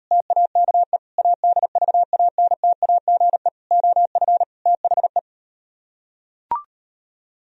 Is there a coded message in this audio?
Yes: Morse code from 0.1 to 5.2 s, heard as a keyed tone, about 38 wpm, 700 Hz. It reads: TAKE ADVANTAGE OF THE